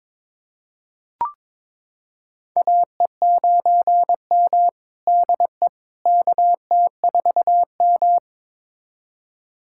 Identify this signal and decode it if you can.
Morse code, heard as a keyed tone, about 22 wpm, 700 Hz. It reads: AE9M DE KT4M
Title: USA Callsign DE Callsign - 4 Characters - 22wpm